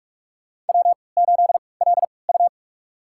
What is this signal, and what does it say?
Morse code, heard as a keyed tone, about 44 wpm, spaced as for 25 wpm, 690 Hz. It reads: W8LU
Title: USA Call Signs - 4 Characters - 3x Character Spacing - 45wpm